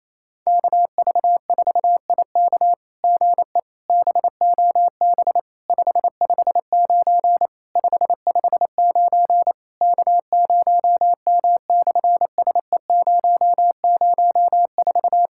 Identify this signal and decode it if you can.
Morse code, heard as a keyed tone, about 28 wpm, 700 Hz. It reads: KV4IK GE BOB 559 559 K0M/SE004
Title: SOTA QSO Practice 28wpm